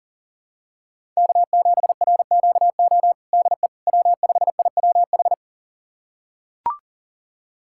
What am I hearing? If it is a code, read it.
Morse code, heard as a keyed tone, about 40 wpm, 700 Hz. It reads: K7RQO DE W5IWH
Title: USA Callsign DE Callsign - 5 Characters - 40wpm